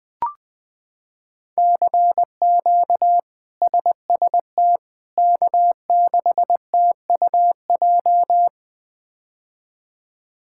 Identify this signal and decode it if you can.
Morse code, heard as a keyed tone, about 21 wpm, 700 Hz. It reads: CQ SST K6TUJ
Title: K1USN SST QSO Practice - 20wpm